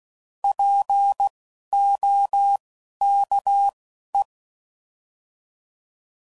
Morse code, heard as a keyed tone, about 16 wpm, spaced as for 11 wpm, 800 Hz. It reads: POKE
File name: Prefixes / By First Letter / Ps